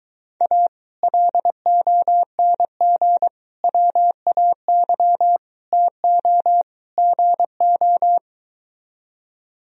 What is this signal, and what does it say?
Morse code, heard as a keyed tone, about 22 wpm, 700 Hz. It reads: A LONG WAY TO GO